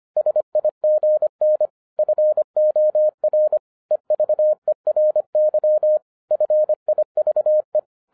Morse code, heard as a keyed tone, about 26 wpm, 610 Hz. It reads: SIGN FOR EVERY FIVE
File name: 2012-01-mp3 / n08-ITS MY FAULT, BUT-25wpm